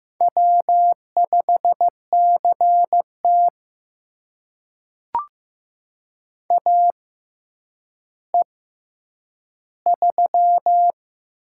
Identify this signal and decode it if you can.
Morse code, heard as a keyed tone, about 15 wpm, spaced as for 9 wpm, 700 Hz. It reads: W5CT AE3
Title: USA Call Signs - 4 Characters - 6x Character Spacing - 15wpm